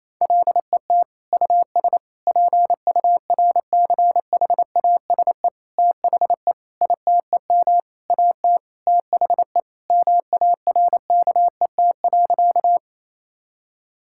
Morse code, heard as a keyed tone, about 28 wpm, 700 Hz. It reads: LET US PURCHASE THE ITEM AT THE MARKET.
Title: Sentences from Top 1000 Words - Review Part 6 - 28wpm